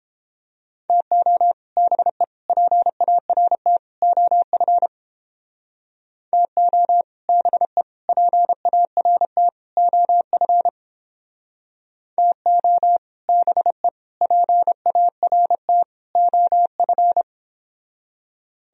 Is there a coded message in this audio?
Yes: Morse code from 0.9 to 17.2 s, heard as a keyed tone, about 29 wpm, 700 Hz. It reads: TO BE PART OF TO BE PART OF TO BE PART OF